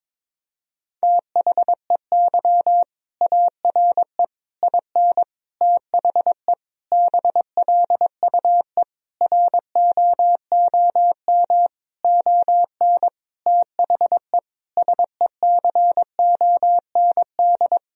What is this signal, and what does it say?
Morse code, heard as a keyed tone, about 22 wpm, 700 Hz. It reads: THEY ARE IN THE BLUE ROOM ON THE SECOND
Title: Sentences from Top 1000 Words - Review Part 3 - 22wpm